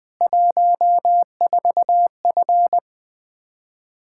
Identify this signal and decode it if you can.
Morse code, heard as a keyed tone, about 20 wpm, 700 Hz. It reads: 14F
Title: K1USN SST QSO Practice 20wpm